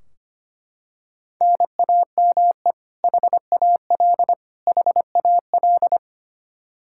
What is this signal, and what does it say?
Morse code, heard as a keyed tone, about 25 wpm, 700 Hz. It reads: NAME HAL HAL
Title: Practice QSO - Exchange 1 - Beginning - 25wpm